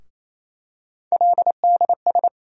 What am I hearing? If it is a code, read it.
Morse code, heard as a keyed tone, about 28 wpm, 700 Hz. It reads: LDS
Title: USA Call Sign Postfixes - 3 Letters 28wpm